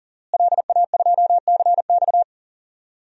Morse code, heard as a keyed tone, about 40 wpm, 700 Hz. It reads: LA2CX
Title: DXCC-100 Call Signs - 5 Characters - Repeated 2x - 40wpm